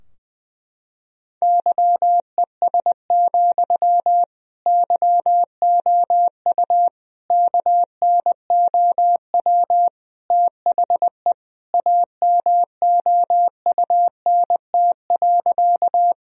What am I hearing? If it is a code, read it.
Morse code, heard as a keyed tone, about 21 wpm, 700 Hz. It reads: YES, YOU KNOW THE AMOUNT.
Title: Sentences from Top 1000 Words - Review Part 4 - 20wpm